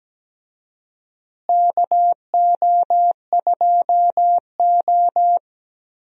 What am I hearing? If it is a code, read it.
Morse code, heard as a keyed tone, about 17 wpm, 700 Hz. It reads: KO2O
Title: USA Call Signs - 4 Characters - Repeated 2x - 17wpm